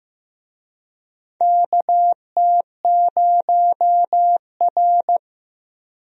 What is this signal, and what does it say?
Morse code, heard as a keyed tone, about 15 wpm, 700 Hz. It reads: KT0R
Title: USA Call Signs - 4 Characters - Repeated 3x - 15wpm